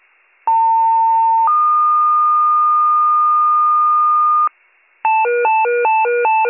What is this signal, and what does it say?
Two-tone page, heard as a one-second tone, then a three-second tone, about 900 then 1220 Hz. It is heard from 0.5 to 4.5 s.